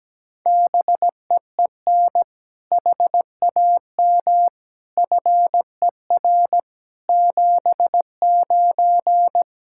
Morse code, heard as a keyed tone, about 17 wpm, 700 Hz. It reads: BEEN HAM FER 79
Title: Practice QSO - Exchange 3 - More Info - 17wpm